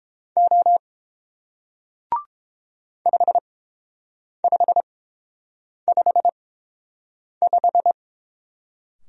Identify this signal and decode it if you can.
Morse code, heard as a keyed tone, about 30 wpm, spaced as for 7 wpm, 700 Hz. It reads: O 5555